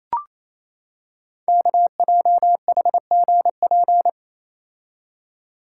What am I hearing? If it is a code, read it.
Morse code, heard as a keyed tone, about 28 wpm, 700 Hz. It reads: KJHGP
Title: Code Groups - 5 Letters - 28wpm